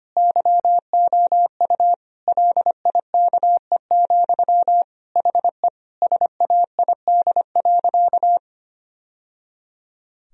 Morse code, heard as a keyed tone, about 25 wpm, 700 Hz. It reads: YOU LIKE, HE SAID.
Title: Sentences from Top 1000 Words - Review Part 2 - 25wpm